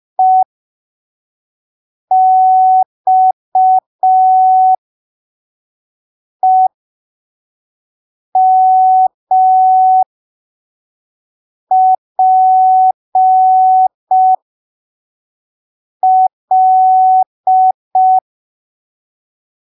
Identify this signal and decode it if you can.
Morse code, heard as a keyed tone, about 5 wpm, spaced as for 3 wpm, 750 Hz. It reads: EXEMPL
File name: arquivos / texto / texto 20 05ppm